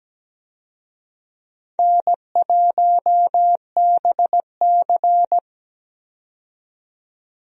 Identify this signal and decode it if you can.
Morse code, heard as a keyed tone, about 17 wpm, 700 Hz. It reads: N1BC
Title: USA Call Signs - 4 Characters - 17wpm